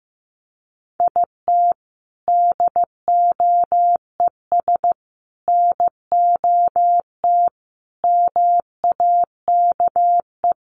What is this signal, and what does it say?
Morse code, heard as a keyed tone, about 15 wpm, 700 Hz. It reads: IT DOES NOT MAKE
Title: Sets of 5 Words Encore2 15wpm